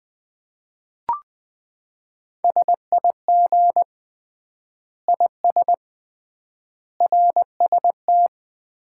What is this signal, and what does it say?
Morse code, heard as a keyed tone, about 20 wpm, 700 Hz. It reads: SIG IS RST
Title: Common QSO Phrases - 3x Word-Spacing - 20wpm